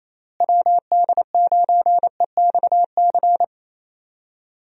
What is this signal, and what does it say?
Morse code, heard as a keyed tone, about 27 wpm, 700 Hz. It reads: WD9EXC
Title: USA Call Signs - 6 Characters - Repeated 3x - 28wpm